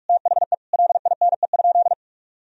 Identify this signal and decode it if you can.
Morse code, heard as a keyed tone, about 46 wpm, 700 Hz. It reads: THE LINE?
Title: Sentences from Top 1000 Words - Review Part 1 - 45wpm